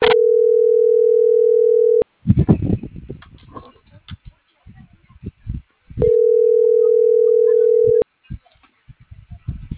Ringback tone (North America), 440 + 480 Hz.
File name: CENTRAL BLN / out ericka.canales - .47220